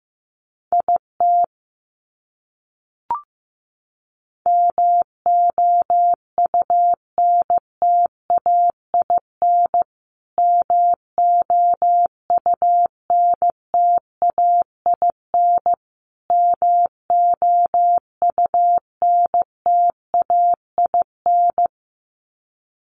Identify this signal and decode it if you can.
Morse code, heard as a keyed tone, about 15 wpm, 700 Hz. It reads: IT MOUNTAIN MOUNTAIN MOUNTAIN
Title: Top 500 Words Repeated 15wpm